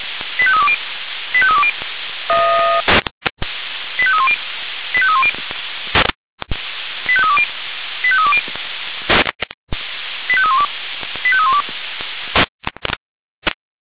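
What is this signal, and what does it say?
Five-tone selective call (ZVEI), heard as quick runs of five tones, tones of about 70 ms. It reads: 85320 85320 85310 85310 85320 85320 85312 85312
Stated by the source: ZVEI five-tone selective call received by SDR, archived by Feuerwehr Orlamünde (fire brigade).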